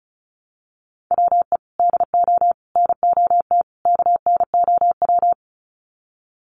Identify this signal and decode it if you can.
Morse code, heard as a keyed tone, about 34 wpm, 700 Hz. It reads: WE DO NOT KNOW
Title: Sets of 4 Words 35wpm